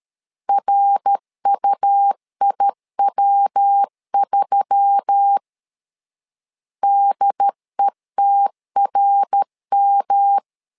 Morse code, heard as a keyed tone, about 13 wpm, 800 Hz. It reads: RUIW3 DETRM
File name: LEZIONI / 4 Q1S3 12wpm 2023